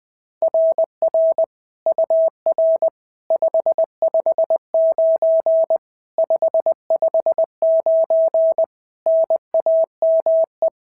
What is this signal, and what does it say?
Morse code, heard as a keyed tone, about 20 wpm, 650 Hz. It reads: RR UR 559 559 NAME
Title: SOTA QSO Practice 20wpm